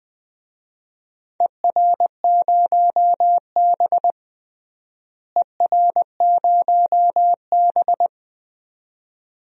Morse code, heard as a keyed tone, about 20 wpm, 700 Hz. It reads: ER0B ER0B
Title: DXCC-100 Call Signs - 4 Characters - Repeated 2x - 20wpm